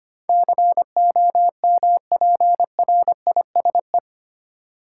Morse code, heard as a keyed tone, about 25 wpm, 700 Hz. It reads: COMPRISE
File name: Eight Letter Words - 25wpm